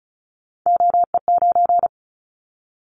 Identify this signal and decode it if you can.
Morse code, heard as a keyed tone, about 34 wpm, 700 Hz. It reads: OE9
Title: Callsign Prefixes for Top 100 DXCC - 35wpm